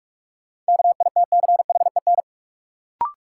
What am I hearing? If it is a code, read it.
Morse code, heard as a keyed tone, about 41 wpm, 700 Hz. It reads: KITCHEN
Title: Sentences from Top 1400 Words - New Word Drill Alt - 45wpm